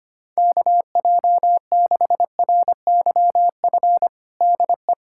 Morse code, heard as a keyed tone, about 25 wpm, 700 Hz. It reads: KJ6RYF DE
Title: USA Callsign DE Callsign - 6 Characters - 25wpm